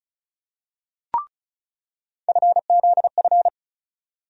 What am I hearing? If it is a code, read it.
Morse code, heard as a keyed tone, about 36 wpm, 700 Hz. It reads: FZF
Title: USA Call Sign Postfixes - 3 Letters 35wpm